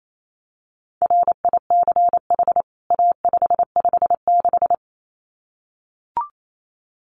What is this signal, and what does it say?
Morse code, heard as a keyed tone, about 28 wpm, 700 Hz. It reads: RICH A556
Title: CWT Exchange 28wpm